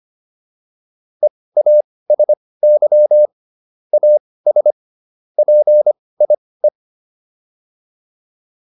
Morse code, heard as a keyed tone, about 25 wpm, spaced as for 18 wpm, 600 Hz. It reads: EASY AS PIE